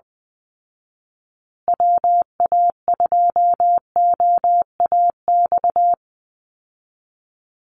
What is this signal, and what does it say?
Morse code, heard as a keyed tone, about 20 wpm, 700 Hz. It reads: WA2OAX
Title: Call Signs Repeated 2x - 20wpm